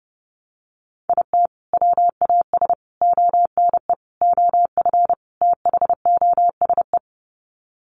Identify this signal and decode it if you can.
Morse code, heard as a keyed tone, about 28 wpm, 700 Hz. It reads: IT WAS ONE OF THOSE
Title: Sets of 5 Words Encore 30wpm